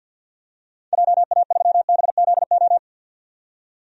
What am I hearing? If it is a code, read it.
Morse code, heard as a keyed tone, about 51 wpm, 700 Hz. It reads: JA3BZO
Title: DXCC-100 Call Signs - 6 Characters - 50wpm